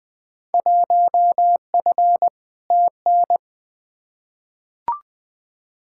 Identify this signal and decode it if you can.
Morse code, heard as a keyed tone, about 21 wpm, 700 Hz. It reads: 1F TN
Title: Field Day Exchanges - 20wpm